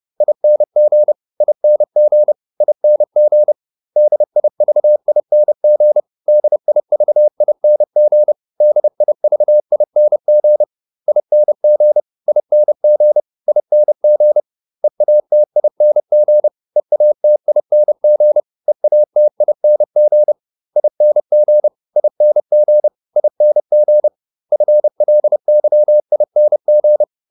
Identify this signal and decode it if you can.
Morse code, heard as a keyed tone, about 30 wpm, 600 Hz. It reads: ING ING ING DIVING DIVING DIVING ING ING ING EATING EATING EATING ING ING ING FLYING